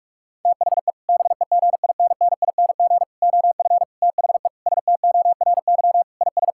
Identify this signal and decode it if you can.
Morse code, heard as a keyed tone, about 46 wpm, 700 Hz. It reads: THE BEGINNING OF THE STORY IS
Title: Sentences from Top 1000 Words - Review Part 6 - 45wpm